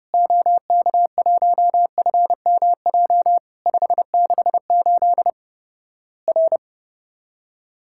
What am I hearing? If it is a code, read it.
Morse code, heard as a keyed tone, about 30 wpm, 700 Hz. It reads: OK1FMJ 568 R
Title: Worked All Europe DX Contest - QTC Exchange - 30wpm